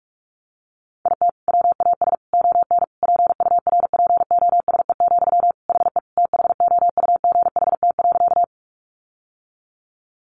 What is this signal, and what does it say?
Morse code, heard as a keyed tone, about 44 wpm, 700 Hz. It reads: IT WAS ON PURPOSE, HE THOUGHT.
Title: Sentences from Top 800 Words 45wpm